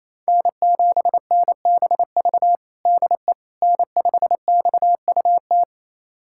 Morse code, heard as a keyed tone, about 28 wpm, 700 Hz. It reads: N7NBV DE N5XUT
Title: USA Callsign DE Callsign - 5 Characters - 28wpm